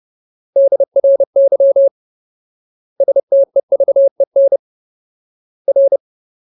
Morse code, heard as a keyed tone, about 30 wpm, 550 Hz. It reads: DRY STEVEN R
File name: Ultimate Word List / 30WPM Ultimate